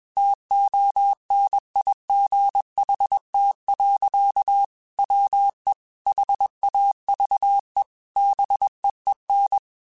Morse code, heard as a keyed tone, about 21 wpm, 800 Hz. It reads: TONIGHT. WE HAVE BEEN
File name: QSOs / QSOs-mp3s / WD0BGP 1@20